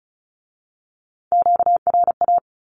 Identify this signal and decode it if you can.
Morse code, heard as a keyed tone, about 36 wpm, 700 Hz. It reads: QRA
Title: Q-Codes with Spoken Meaning - 35wpm